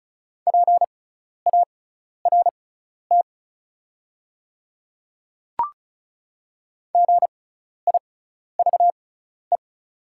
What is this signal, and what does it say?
Morse code, heard as a keyed tone, about 36 wpm, spaced as for 12 wpm, 690 Hz. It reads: PART GIVE
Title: 6x Character Spacing - Top 300 Words - 35wpm